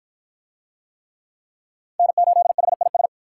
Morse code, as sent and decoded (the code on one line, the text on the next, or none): -. ---.. .... .. ...
N8HIS